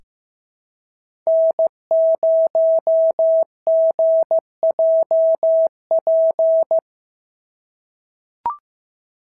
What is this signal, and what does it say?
Morse code, heard as a keyed tone, about 15 wpm, 650 Hz. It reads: N0GJP
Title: SOTA QSO Practice 15wpm